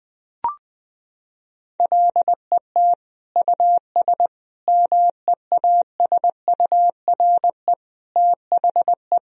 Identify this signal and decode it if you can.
Morse code, heard as a keyed tone, about 20 wpm, 700 Hz. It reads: LET US MEASURE THE
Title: Sentences from Top 1000 Words - Review Part 2 - 20wpm